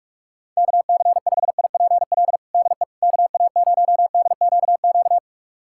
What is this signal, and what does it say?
Morse code, heard as a keyed tone, about 41 wpm, 700 Hz. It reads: KC5IPL DE KA0DQQ
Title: USA Callsign DE Callsign - 6 Characters - 45wpm